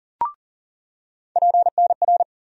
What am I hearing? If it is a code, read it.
Morse code, heard as a keyed tone, about 40 wpm, 700 Hz. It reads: PNR
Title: USA Call Sign Postfixes - 3 Letters 40wpm